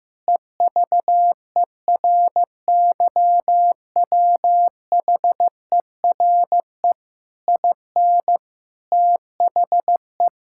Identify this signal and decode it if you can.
Morse code, heard as a keyed tone, about 15 wpm, 700 Hz. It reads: EVERYWHERE IN THE
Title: Sentences from Top 1600 Words 15wpm